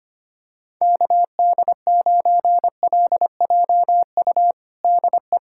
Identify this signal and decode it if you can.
Morse code, heard as a keyed tone, about 25 wpm, 700 Hz. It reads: KD9LJU DE
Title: USA Callsign DE Callsign 25wpm